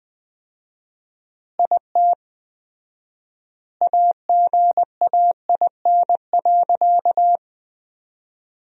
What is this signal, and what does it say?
Morse code, heard as a keyed tone, about 20 wpm, 700 Hz. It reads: IT AGAIN.
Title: 4x Word-Spacing - Sentences from Top 300 Words - 20wpm